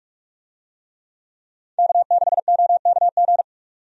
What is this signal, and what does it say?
Morse code, heard as a keyed tone, about 47 wpm, 700 Hz. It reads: K6OKG
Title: USA Call Signs - 5 Characters - 45wpm